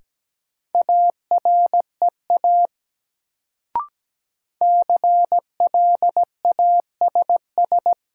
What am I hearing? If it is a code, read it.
Morse code, heard as a keyed tone, about 17 wpm, 700 Hz. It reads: AREA CLASS
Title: Sentences from Top 1000 Words - Review Part 2 - New Word Drill - 17wpm